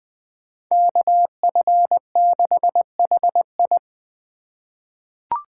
Morse code, heard as a keyed tone, about 20 wpm, 700 Hz. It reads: KF6HI
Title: 2023 SOTA Chaser Call Signs - 20wpm